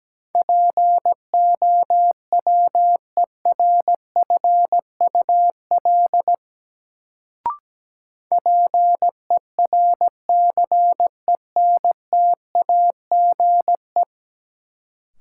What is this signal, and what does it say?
Morse code, heard as a keyed tone, about 17 wpm, 700 Hz. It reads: POWERFUL PERCENTAGE
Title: Sentences from Top 1300 Words - New Word Drill Alt - 17wpm